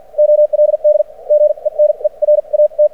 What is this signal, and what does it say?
Morse code, heard as a keyed tone, about 53 wpm, spaced as for 41 wpm, 600 Hz. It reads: OLD GENERAT
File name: archived / CRSnet